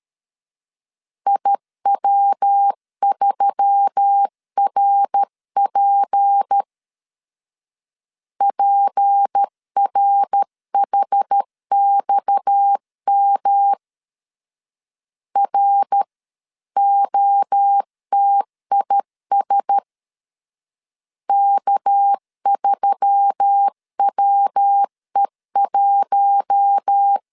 Morse code, heard as a keyed tone, about 13 wpm, 800 Hz. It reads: IW3RP PRHXM ROTIS K3WE1
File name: CW / DET / 1 6 1500C S C 12wpm 3533